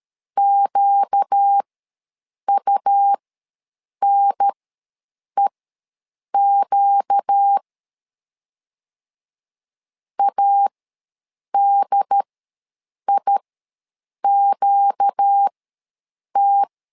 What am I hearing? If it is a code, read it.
Morse code, heard as a keyed tone, about 13 wpm, spaced as for 7 wpm, 800 Hz. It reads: QUNEQ ADIQT
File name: LEZIONI / 4 Q ws 12wpm 1912